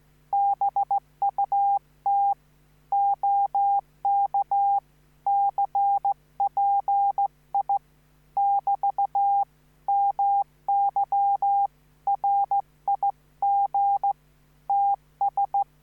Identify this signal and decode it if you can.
Morse code, heard as a keyed tone, about 15 wpm, 800 Hz. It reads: BUT OK CPI <BT> MY RIG TS